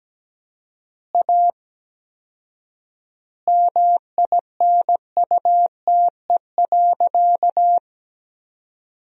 Morse code, heard as a keyed tone, about 17 wpm, 700 Hz. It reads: A MINUTE.
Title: 4x Word-Spacing - Sentences from Top 500 Words - 17wpm